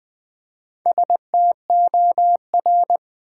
Morse code, heard as a keyed tone, about 20 wpm, 700 Hz. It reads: STOR
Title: Build A Word - 1 Letter At A Time - 3 to 5-Letter Words - 20wpm